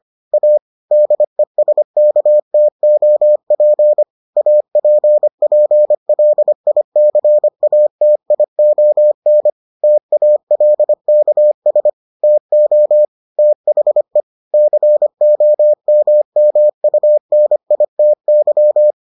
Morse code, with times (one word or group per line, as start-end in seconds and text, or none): 0.3-0.6 A
0.9-4.0 DESKTOP
4.4-9.5 APPLICATION
9.8-11.9 TALKS
12.2-13.1 TO
13.4-14.2 THE
14.5-18.9 COMMUNITY